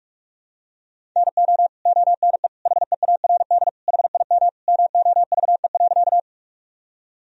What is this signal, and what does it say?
Morse code, heard as a keyed tone, about 46 wpm, 700 Hz. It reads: NO ONE HEARD HIM MOVE.
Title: Sentences from Top 1000 Words - Review Part 1 - 45wpm